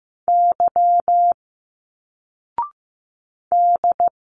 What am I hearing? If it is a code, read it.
Morse code, heard as a keyed tone, about 15 wpm, 700 Hz. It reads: Y D
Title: Single Letters 15wpm